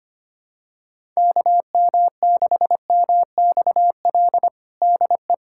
Morse code, heard as a keyed tone, about 25 wpm, 700 Hz. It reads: KM6MXL DE
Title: USA Callsign DE Callsign 25wpm